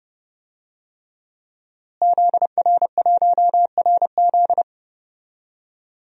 Morse code, heard as a keyed tone, about 31 wpm, 700 Hz. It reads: ZR1RZ
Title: DXCC-100 Call Signs - 5 Characters - Repeated 3x - 30wpm